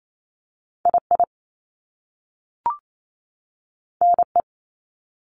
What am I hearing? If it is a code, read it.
Morse code, heard as a keyed tone, about 28 wpm, 700 Hz. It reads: I I N E
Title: Top Two Letter Combinations in Words 28wpm